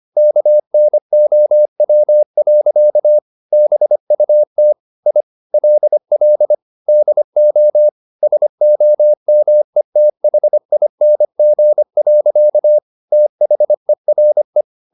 Morse code, heard as a keyed tone, about 25 wpm, 600 Hz. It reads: KNOW. BUT I LL DO SOMETHING. THERE
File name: the old man and the sea part 1 25wpm